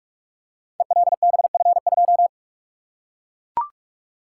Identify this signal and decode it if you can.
Morse code, heard as a keyed tone, about 44 wpm, 700 Hz. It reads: ELBF2
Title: Code Groups - 5 Letter:Numbers - 45wpm